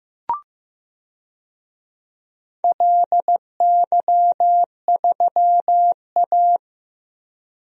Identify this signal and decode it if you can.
Morse code, heard as a keyed tone, about 15 wpm, 700 Hz. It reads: LY3A
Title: DXCC-100 Call Signs - 4 Characters - Repeated 2x - 15wpm